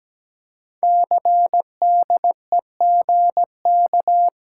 Morse code, heard as a keyed tone, about 17 wpm, 700 Hz. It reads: CDEGK
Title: Code Groups - 5 Letters - 17wpm